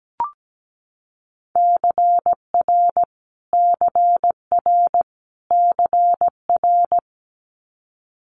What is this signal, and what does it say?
Morse code, heard as a keyed tone, about 17 wpm, 700 Hz. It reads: CR CR CR
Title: Multi-letter Phonemes Repeated 3x 17wpm